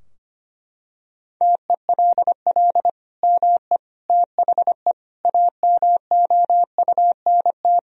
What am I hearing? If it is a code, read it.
Morse code, heard as a keyed tone, about 25 wpm, 700 Hz. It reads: TELL ME THE AMOUNT
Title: Sentences from Top 1000 Words - Review Part 4 - 25wpm